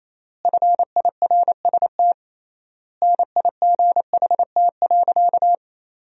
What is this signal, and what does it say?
Morse code, heard as a keyed tone, about 28 wpm, 700 Hz. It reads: FIRST NIGHT.
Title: 3x Word-Spacing - Sentences from Top 300 Words - 28wpm